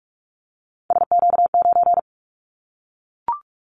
Morse code, heard as a keyed tone, about 44 wpm, 700 Hz. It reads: SQ9
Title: Callsign Prefixes for Top 25 DXCC - Repeated 3x - 45wpm